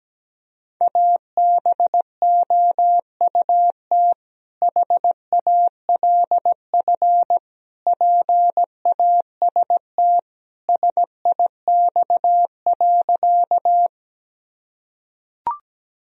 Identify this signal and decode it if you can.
Morse code, heard as a keyed tone, about 17 wpm, 700 Hz. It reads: ABOUT HALF PAST SIX.
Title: Sentences from Top 1000 Words - Review Part 3 - 17wpm